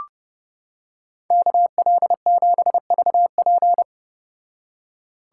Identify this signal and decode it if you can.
Morse code, heard as a keyed tone, about 31 wpm, 700 Hz. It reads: KL7VP